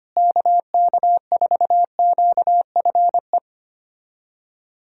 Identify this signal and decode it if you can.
Morse code, heard as a keyed tone, about 25 wpm, 700 Hz. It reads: KK4QFE
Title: USA Callsign DE Callsign - 6 Characters - 25wpm